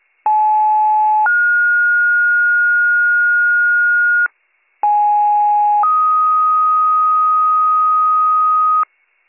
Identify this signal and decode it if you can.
Two-tone page, heard as a one-second tone, then a three-second tone, about 860 then 1430 Hz, about 830 then 1250 Hz, with noise. It is heard from 0.3 to 8.8 s.